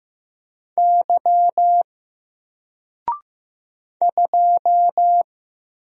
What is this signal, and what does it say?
Morse code, heard as a keyed tone, about 15 wpm, 700 Hz. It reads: Y 2
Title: Single Character (Letter:Number:Punctuation:Prosign) Alt 15wpm